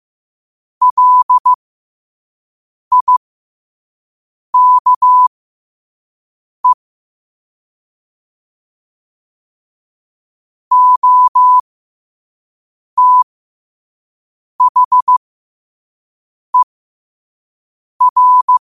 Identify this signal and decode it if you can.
Morse code, heard as a keyed tone, about 15 wpm, spaced as for 5 wpm, 1000 Hz. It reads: LIKE OTHER
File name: qotdfiles / 5wpm